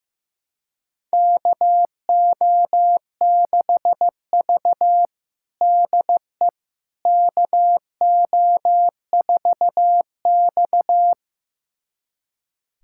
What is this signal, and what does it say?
Morse code, heard as a keyed tone, about 16 wpm, 700 Hz. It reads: KO6V DE KO4X
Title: USA Callsign DE Callsign - 4 Characters - 15wpm